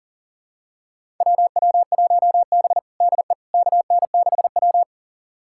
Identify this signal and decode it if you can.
Morse code, heard as a keyed tone, about 40 wpm, 700 Hz. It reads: WW1B DE KN6W
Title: USA Callsign DE Callsign - 4 Characters - 40wpm